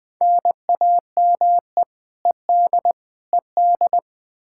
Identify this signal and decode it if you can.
Morse code, heard as a keyed tone, about 20 wpm, 700 Hz. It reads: NAME ED ED
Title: Practice QSO - Exchange 1 - Beginning - 20wpm